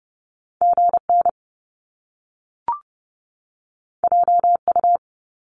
Morse code, heard as a keyed tone, about 30 wpm, 700 Hz. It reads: GN JU
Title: Top Two Letter Combinations in Words 30wpm